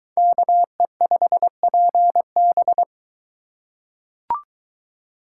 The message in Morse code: KE5PB

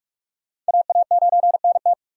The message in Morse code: AA9NT

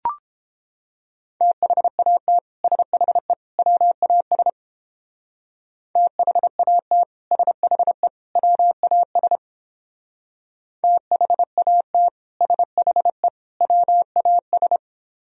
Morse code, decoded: THAT SHE WAS THAT SHE WAS THAT SHE WAS

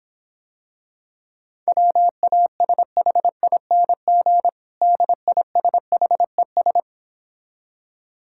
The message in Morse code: WASHING DISHES

